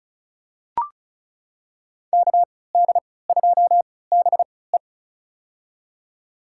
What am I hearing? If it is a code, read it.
Morse code, heard as a keyed tone, about 34 wpm, spaced as for 20 wpm, 700 Hz. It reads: KD2BE
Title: USA Call Signs - 5 Characters - 3x Character Spacing - 35wpm